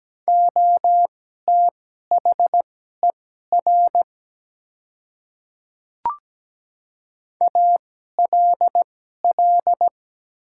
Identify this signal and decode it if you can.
Morse code, heard as a keyed tone, about 17 wpm, spaced as for 12 wpm, 700 Hz. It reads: OTHER ALL
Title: 2x Character Spacing - Top 100 Words - 17wpm